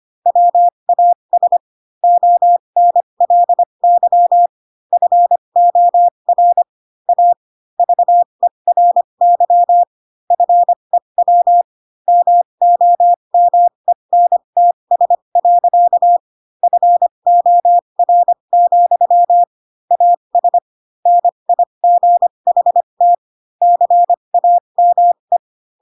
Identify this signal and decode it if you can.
Morse code, heard as a keyed tone, about 25 wpm, spaced as for 22 wpm, 700 Hz. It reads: WAS ONLY FOR A VERY FEW MOMENTS. FOR, AS NIGHT CAME